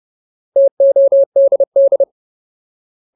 Morse code, heard as a keyed tone, about 30 wpm, 550 Hz. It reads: TODD
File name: Ultimate Word List / 30WPM Ultimate